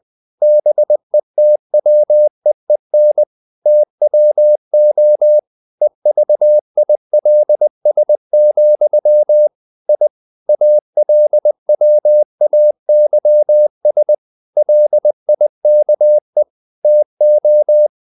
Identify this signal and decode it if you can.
Morse code, heard as a keyed tone, about 20 wpm, 600 Hz. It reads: BETWEEN TWO EVILS, I ALWAYS LIKE TO
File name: CW / Practice / EnglishQuotations0320